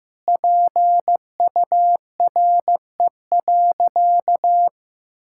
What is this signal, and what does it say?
Morse code, heard as a keyed tone, about 16 wpm, 700 Hz. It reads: PURE.
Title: Sentences from Top 1600 Words 15wpm